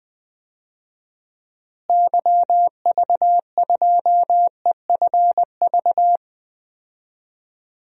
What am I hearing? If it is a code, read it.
Morse code, heard as a keyed tone, about 20 wpm, 690 Hz. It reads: YV2EFV
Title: DXCC-100 Call Signs - 6 Characters - Repeated 2x - 20wpm